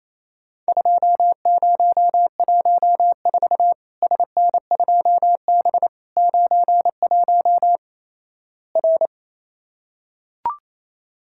Morse code, heard as a keyed tone, about 28 wpm, 700 Hz. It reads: 2014 SN2B 91 R
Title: Worked All Europe DX Contest - QTC Exchange - 28wpm